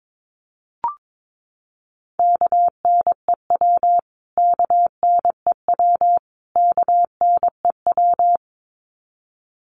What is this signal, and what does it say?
Morse code, heard as a keyed tone, about 22 wpm, 700 Hz. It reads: KNEW KNEW KNEW
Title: Top 500 Words Repeated 22wpm